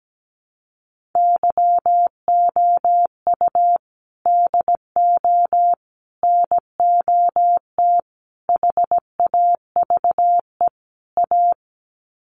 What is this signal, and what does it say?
Morse code, heard as a keyed tone, about 17 wpm, 700 Hz. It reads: YOU DO NOT HAVE A
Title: Sets of 5 Words 17wpm